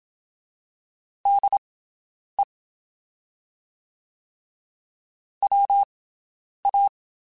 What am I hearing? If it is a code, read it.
Morse code, heard as a keyed tone, about 27 wpm, spaced as for 9 wpm, 800 Hz. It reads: DE WA
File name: QSOs Spaced